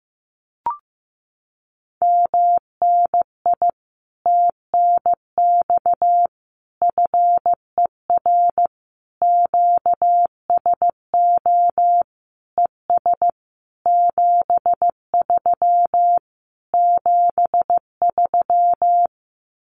Morse code, heard as a keyed tone, about 15 wpm, 700 Hz. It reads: MNI TNX FER QSO ES 73 73
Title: Common QSO Phrases Encore 15wpm